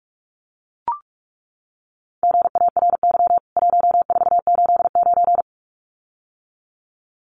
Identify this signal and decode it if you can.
Morse code, heard as a keyed tone, about 44 wpm, 700 Hz. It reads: GARY 1489
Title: CWT Exchange 45wpm